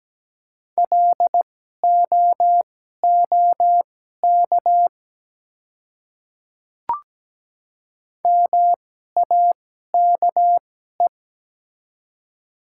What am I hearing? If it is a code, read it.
Morse code, heard as a keyed tone, about 17 wpm, spaced as for 12 wpm, 700 Hz. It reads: LOOK MAKE
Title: 2x Character Spacing - Top 100 Words - 17wpm